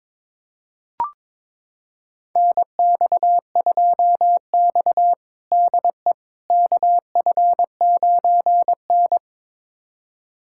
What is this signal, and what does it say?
Morse code, heard as a keyed tone, about 22 wpm, 700 Hz. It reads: NX2X DE KF9N
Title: USA Callsign DE Callsign - 4 Characters - 22wpm